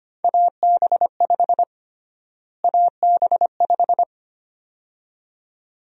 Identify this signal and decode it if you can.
Morse code, heard as a keyed tone, about 25 wpm, 700 Hz. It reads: AB5 AB5
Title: USA Call Sign Prefixes - Repeated 2x - 25wpm